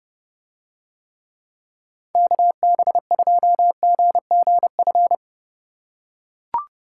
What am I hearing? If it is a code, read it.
Morse code, heard as a keyed tone, about 30 wpm, 700 Hz. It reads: KB2GGF